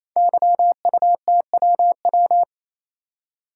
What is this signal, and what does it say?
Morse code, heard as a keyed tone, about 29 wpm, 700 Hz. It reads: YUTWW
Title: Code Groups - 5 Letters - 28wpm